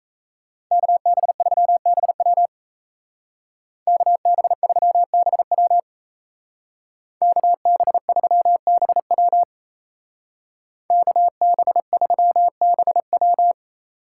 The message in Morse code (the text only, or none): KB3BW KB3BW KB3BW KB3BW